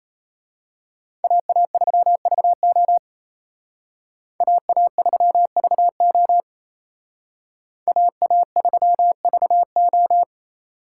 Morse code, heard as a keyed tone, about 32 wpm, 700 Hz. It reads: AA3VO AA3VO AA3VO
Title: Speed Racing - USA 2x2 Call Signs 28wpm